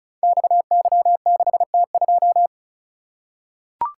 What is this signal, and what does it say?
Morse code, heard as a keyed tone, about 36 wpm, 700 Hz. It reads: XY6T2
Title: Code Groups - 5 Letter:Numbers - 35wpm